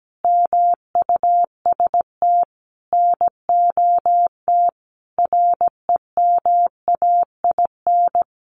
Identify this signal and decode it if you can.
Morse code, heard as a keyed tone, about 17 wpm, 700 Hz. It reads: MUST NOT REMAIN
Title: Sentences from Top 1200 Words 17wpm